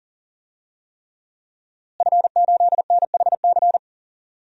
Morse code, heard as a keyed tone, about 40 wpm, 700 Hz. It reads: F8NHC
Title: DXCC-100 Call Signs - 5 Characters - Repeated 3x - 40wpm